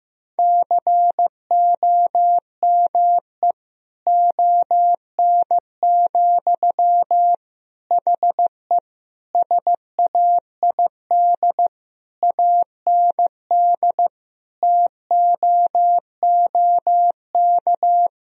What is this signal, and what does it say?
Morse code, heard as a keyed tone, about 15 wpm, 700 Hz. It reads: COME ON, HE SAID AND TOOK
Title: Sentences from Top 1000 Words - Review Part 2 - 15wpm